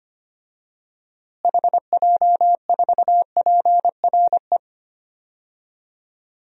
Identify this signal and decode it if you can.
Morse code, heard as a keyed tone, about 25 wpm, 700 Hz. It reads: HJ4PRE
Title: DXCC-100 Call Signs - 6 Characters - Repeated 2x - 25wpm